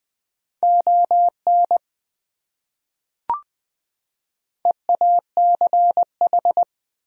Morse code, heard as a keyed tone, about 20 wpm, 700 Hz. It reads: ON EACH
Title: Top 100 Words 20wpm